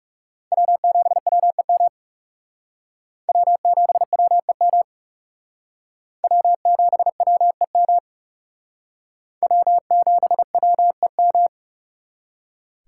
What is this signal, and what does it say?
Morse code, heard as a keyed tone, about 37 wpm, 700 Hz. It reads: W7WEM W7WEM W7WEM W7WEM
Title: Speed Racing - USA 1x3 Call Signs 30wpm